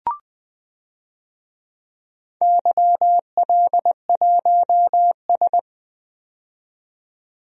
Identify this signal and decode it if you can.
Morse code, heard as a keyed tone, about 20 wpm, 700 Hz. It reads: YL1S
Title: DXCC-100 Call Signs - 4 Characters - 20wpm